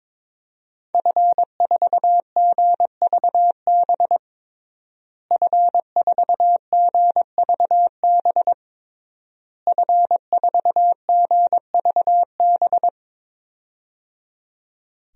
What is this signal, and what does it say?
Morse code, heard as a keyed tone, about 21 wpm, 700 Hz. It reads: F4GVB F4GVB F4GVB
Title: DXCC-100 Call Signs - 5 Characters - Repeated 3x - 22wpm